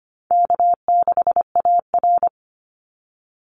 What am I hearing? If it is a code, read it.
Morse code, heard as a keyed tone, about 25 wpm, 700 Hz. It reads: K6AR
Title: Easy Call Signs Repeated 2x - 25wpm